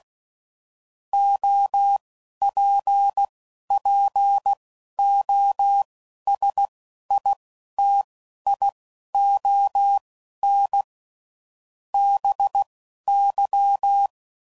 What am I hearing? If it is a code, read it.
Morse code, heard as a keyed tone, about 16 wpm, spaced as for 11 wpm, 800 Hz. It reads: OPPOSITION BY
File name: CW Code Groups @10WPM / Technical Subjects / XC